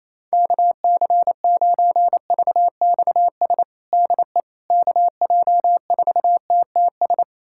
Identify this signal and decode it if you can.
Morse code, heard as a keyed tone, about 28 wpm, 700 Hz. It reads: KC9VXS DE KJ4TTS
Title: USA Callsign DE Callsign - 6 Characters - 28wpm